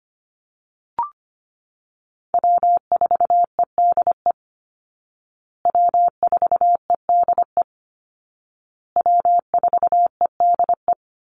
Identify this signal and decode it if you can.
Morse code, heard as a keyed tone, about 25 wpm, 700 Hz. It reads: W4EDE W4EDE W4EDE
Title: Call Signs Repeated 2x - 25wpm